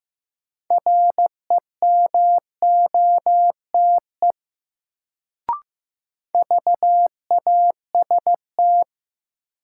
Morse code, heard as a keyed tone, about 15 wpm, 700 Hz. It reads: REMOTE VAST
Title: Sentences from Top 1300 Words - New Word Drill Alt - 15wpm